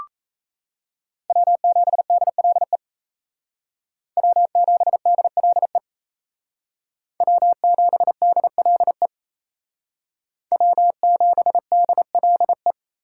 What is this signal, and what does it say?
Morse code, heard as a keyed tone, about 36 wpm, 700 Hz. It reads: W7DLE W7DLE W7DLE W7DLE